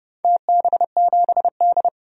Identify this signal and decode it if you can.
Morse code, heard as a keyed tone, about 30 wpm, 700 Hz. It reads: TB7D
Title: DXCC-100 Call Signs - 4 Characters - 30wpm